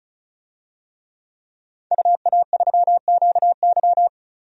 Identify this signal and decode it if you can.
Morse code, heard as a keyed tone, about 36 wpm, 700 Hz. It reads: UA3QY
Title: DXCC-100 Call Signs - 5 Characters - 35wpm